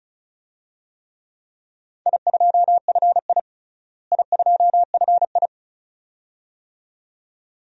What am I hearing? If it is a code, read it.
Morse code, heard as a keyed tone, about 34 wpm, 700 Hz. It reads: I2FI I2FI
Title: DXCC-100 Call Signs - 4 Characters - Repeated 2x - 35wpm